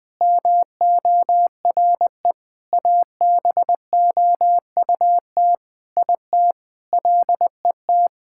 Morse code, heard as a keyed tone, about 20 wpm, 700 Hz. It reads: MORE ABOUT IT LET
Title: Sentences with 8-Word Phrases - Spoken Beginning - 20wpm